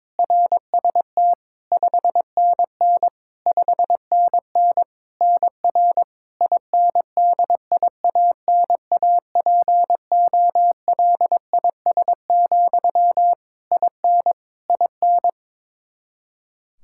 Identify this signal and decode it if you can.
Morse code, heard as a keyed tone, about 22 wpm, 700 Hz. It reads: RST 5NN 5NN NR INDIANAPOLIS, IN IN
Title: K1USN SST QSO Practice 22wpm